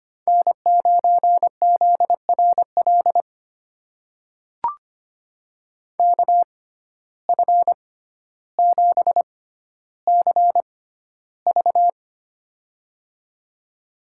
Morse code, heard as a keyed tone, about 25 wpm, spaced as for 13 wpm, 700 Hz. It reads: N9ZRL KF7CV
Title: USA Call Signs - 5 Characters - 6x Character Spacing - 25wpm